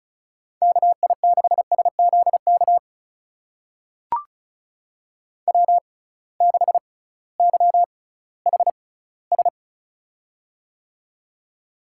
Morse code, heard as a keyed tone, about 36 wpm, spaced as for 19 wpm, 700 Hz. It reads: KI6SZK W6YHS
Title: USA Call Signs - 6x Character Spacing - 35wpm